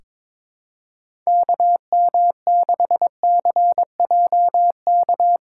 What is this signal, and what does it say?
Morse code, heard as a keyed tone, about 22 wpm, 700 Hz. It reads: KM6CJK
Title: USA Callsign DE Callsign - 6 Characters - 22wpm